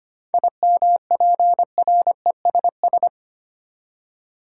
Morse code, heard as a keyed tone, about 25 wpm, 700 Hz. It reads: IMPRESS
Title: Seven Letter Words - 25wpm